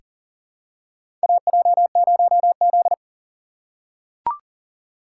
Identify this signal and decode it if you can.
Morse code, heard as a keyed tone, about 40 wpm, 700 Hz. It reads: AJ0Z
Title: USA Call Signs - 4 Characters - 6x Character Spacing - 40wpm